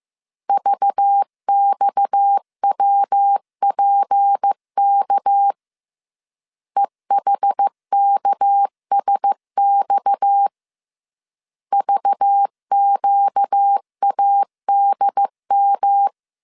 Morse code, heard as a keyed tone, about 15 wpm, 800 Hz. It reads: VXWPK EHKSX VQADM
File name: CW / DET / 1 6 1500C S B 14wpm 2958